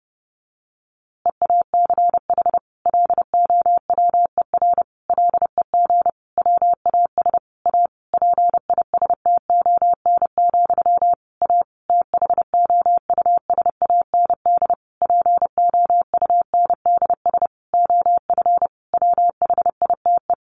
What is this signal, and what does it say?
Morse code, heard as a keyed tone, about 30 wpm, 700 Hz. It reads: EACH LOWER LEG WAS A PISTON, A THOUSAND POUNDS OF WHITE
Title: A Sound Of Thunder 30wpm